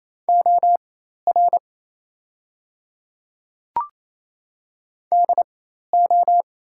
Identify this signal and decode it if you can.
Morse code, heard as a keyed tone, about 28 wpm, spaced as for 13 wpm, 700 Hz. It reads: OR DO
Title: 4x Character Spacing - Top 200 Words - 28wpm